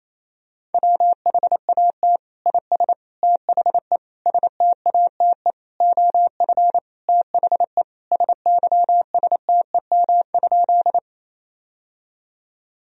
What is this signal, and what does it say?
Morse code, heard as a keyed tone, about 28 wpm, 700 Hz. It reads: WHAT IS THE STATE OF THE SYSTEM?
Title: Sentences from Top 1000 Words - Review Part 3 - 28wpm